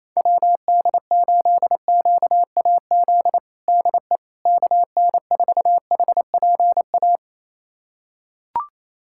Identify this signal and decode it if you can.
Morse code, heard as a keyed tone, about 28 wpm, 700 Hz. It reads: WD8QAZ DE KN4HPA
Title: USA Callsign DE Callsign - 6 Characters - 28wpm